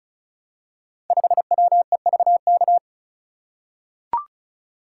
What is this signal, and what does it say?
Morse code, heard as a keyed tone, about 34 wpm, 700 Hz. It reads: 5WEVK
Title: Code Groups - 5 Letter:Numbers - 35wpm